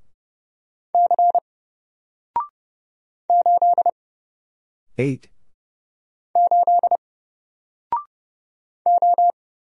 Morse code, heard as a keyed tone, about 30 wpm, 700 Hz. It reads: C 8 8 O
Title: Single Letter-Number 30wpm